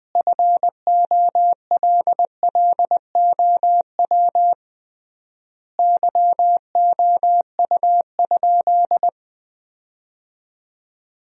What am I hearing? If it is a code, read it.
Morse code, heard as a keyed tone, about 21 wpm, 700 Hz. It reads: FOLLOW YOU?
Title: 3x Word-Spacing - Sentences from Top 200 Words - 20wpm